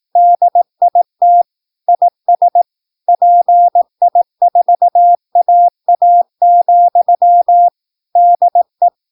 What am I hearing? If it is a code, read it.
Morse code, heard as a keyed tone, about 18 wpm, 700 Hz. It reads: DIT IS PI4AA, DE
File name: PI4AA Uitzending